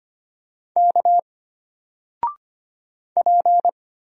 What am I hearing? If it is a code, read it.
Morse code, heard as a keyed tone, about 25 wpm, 700 Hz. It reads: K P
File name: Single-Letter-Number-25wpm